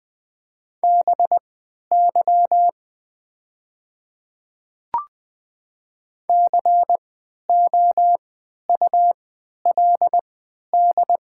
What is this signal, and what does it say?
Morse code, heard as a keyed tone, about 20 wpm, spaced as for 11 wpm, 700 Hz. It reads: BY COULD
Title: 3x Character Spacing - Top 100 Words - 20wpm